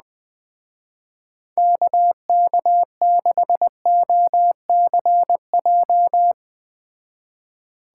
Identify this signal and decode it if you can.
Morse code, heard as a keyed tone, about 20 wpm, 700 Hz. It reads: KK6OCJ
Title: Speed Racing - USA Call Signs 17wpm